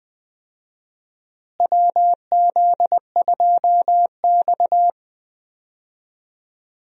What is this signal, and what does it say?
Morse code, heard as a keyed tone, about 20 wpm, 700 Hz. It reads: WZ2X
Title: USA Call Signs - 4 Characters - Repeated 2x - 20wpm